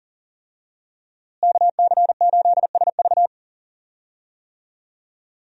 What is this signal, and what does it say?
Morse code, heard as a keyed tone, about 38 wpm, 700 Hz. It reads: KC8SV
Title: USA Call Signs - 5 Characters - Repeated 3x - 40wpm